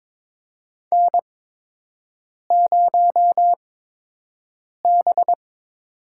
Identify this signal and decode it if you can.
Morse code, heard as a keyed tone, about 23 wpm, spaced as for 6 wpm, 700 Hz. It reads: N0B
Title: USA Call Signs - 5 Characters - 8x Character Spacing - 22wpm